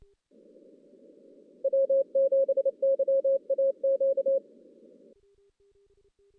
Morse code, heard as a keyed tone, about 30 wpm, 540 Hz. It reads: W7YAQ
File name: 2016 JAN CW NAQP / 2016 Jan naqp k5ot - 0-6